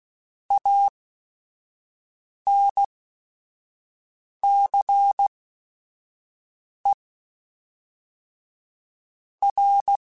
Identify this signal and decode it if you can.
Morse code, heard as a keyed tone, about 16 wpm, spaced as for 5 wpm, 800 Hz. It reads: ANCE R
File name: Technical Subjects / Technical Subjects-mp3s / Reactance @5